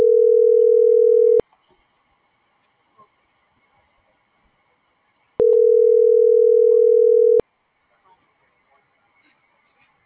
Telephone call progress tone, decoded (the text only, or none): ringback tone